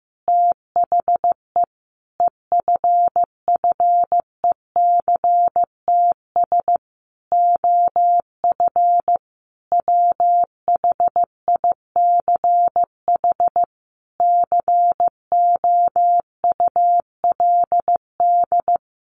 Morse code, heard as a keyed tone, about 15 wpm, 700 Hz. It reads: THE EFFECTS OF WHICH COULD
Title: A Sound Of Thunder 15wpm